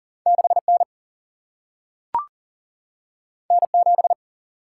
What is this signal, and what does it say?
Morse code, heard as a keyed tone, about 40 wpm, 700 Hz. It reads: 6 N N 7
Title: Single-Letter Single-Number 40wpm